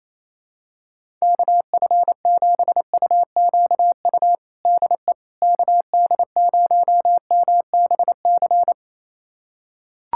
Morse code, heard as a keyed tone, about 27 wpm, 700 Hz. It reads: KF7UQU DE KD0MBC
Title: USA Callsign DE Callsign - 6 Characters - 28wpm